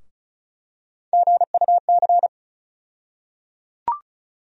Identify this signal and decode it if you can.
Morse code, heard as a keyed tone, about 36 wpm, 700 Hz. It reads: GUC
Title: USA Call Sign Postfixes - 3 Letters 35wpm